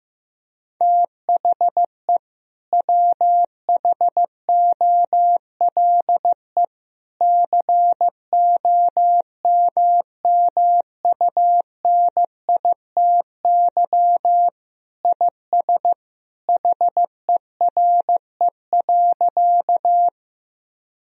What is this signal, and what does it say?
Morse code, heard as a keyed tone, about 15 wpm, 700 Hz. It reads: THE WHOLE COMMUNITY IS HERE.
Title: Sentences from Top 1000 Words - Review Part 4 - 15wpm